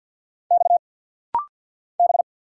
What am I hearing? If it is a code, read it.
Morse code, heard as a keyed tone, about 50 wpm, 700 Hz. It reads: X B
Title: Single Letters - Rapid-Fire 50wpm